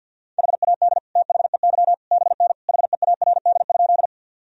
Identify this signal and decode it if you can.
Morse code, heard as a keyed tone, about 51 wpm, 700 Hz. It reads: HAD THEY BN HEARD?
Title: Sentences from Top 400 Words and Abbreviations 50wpm